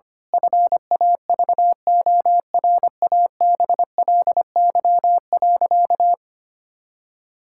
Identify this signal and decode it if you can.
Morse code, heard as a keyed tone, about 25 wpm, 700 Hz. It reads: FAVORABLY.
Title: Pangram Challenge v2 - 25wpm